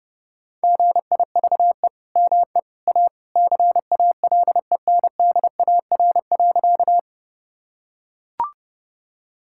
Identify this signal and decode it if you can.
Morse code, heard as a keyed tone, about 31 wpm, 700 Hz. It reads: GIVE ME A CALENDAR.
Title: Sentences from Top 1600 Words 30wpm